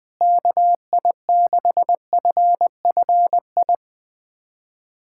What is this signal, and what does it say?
Morse code, heard as a keyed tone, about 21 wpm, 700 Hz. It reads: KI6FFI
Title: USA Callsign DE Callsign - 6 Characters - 20wpm